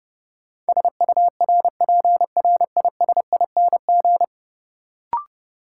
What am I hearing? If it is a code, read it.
Morse code, heard as a keyed tone, about 30 wpm, 700 Hz. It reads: SURPRISING
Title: Sets of 5 Words - New Word Drill - 30wpm